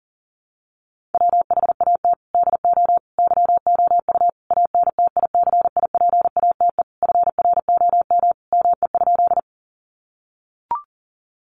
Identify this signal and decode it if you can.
Morse code, heard as a keyed tone, about 40 wpm, 700 Hz. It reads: WHAT DO YOU ANTICIPATE FROM ME?
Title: Sentences from Top 1500 Words 40wpm